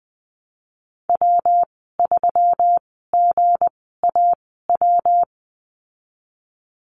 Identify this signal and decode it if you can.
Morse code, heard as a keyed tone, about 19 wpm, spaced as for 14 wpm, 700 Hz. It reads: W3GAW
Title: 186 Lesson 30g - Letter b - Call Signs 20wpm